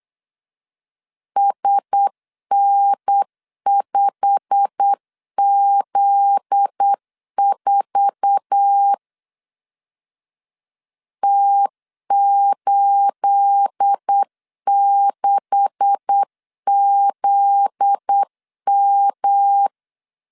Morse code, heard as a keyed tone, about 8 wpm, 800 Hz. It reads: SN5Z4 T86ZM
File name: 8wpm